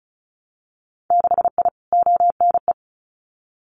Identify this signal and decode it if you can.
Morse code, heard as a keyed tone, about 36 wpm, 700 Hz. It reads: 6I ONE
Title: Winter Field Day Exchanges Encore 35wpm